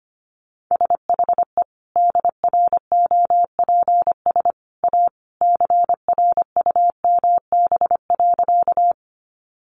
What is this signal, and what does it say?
Morse code, heard as a keyed tone, about 25 wpm, 700 Hz. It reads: SHE DROPS A CRUMB.
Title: Additional Easy Phrases 25wpm